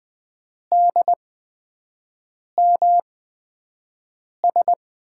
Morse code, heard as a keyed tone, about 20 wpm, spaced as for 5 wpm, 700 Hz. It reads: DMS
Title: USA Call Signs - 6 Characters - 8x Character Spacing - 20wpm